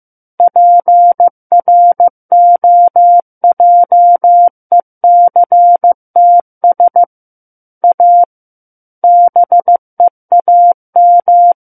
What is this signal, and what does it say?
Morse code, heard as a keyed tone, about 15 wpm, 700 Hz. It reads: PROJECTS A BEAM
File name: Chapter 6 - 15 WPM